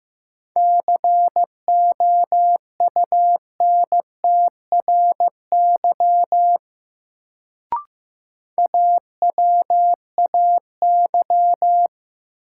Morse code, heard as a keyed tone, about 15 wpm, 700 Hz. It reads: COUNTRY AWAY
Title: Sets of 2 Words - New Word Drill - 15wpm